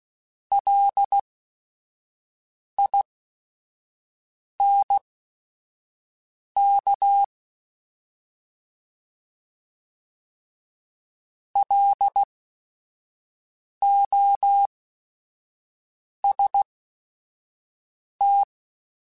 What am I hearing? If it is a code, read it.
Morse code, heard as a keyed tone, about 16 wpm, spaced as for 5 wpm, 800 Hz. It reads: LINK LOST